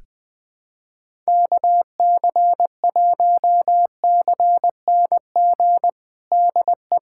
Morse code, heard as a keyed tone, about 20 wpm, 700 Hz. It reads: KC1CNG DE
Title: USA Callsign DE Callsign 20wpm